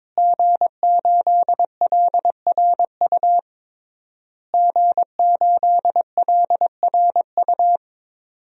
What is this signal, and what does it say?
Morse code, heard as a keyed tone, about 21 wpm, 700 Hz. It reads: G8LRU G8LRU
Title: DXCC-100 Call Signs - 5 Characters - Repeated 3x - 22wpm